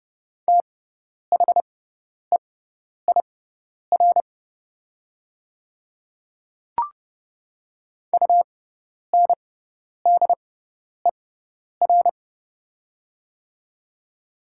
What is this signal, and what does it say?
Morse code, heard as a keyed tone, about 30 wpm, spaced as for 10 wpm, 700 Hz. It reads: THEIR UNDER
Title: 6x Character Spacing - Top 300 Words - 30wpm